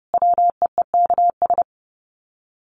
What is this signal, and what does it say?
Morse code, heard as a keyed tone, about 30 wpm, 700 Hz. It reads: WEEKS
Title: Sets of 5 Words Encore2 30wpm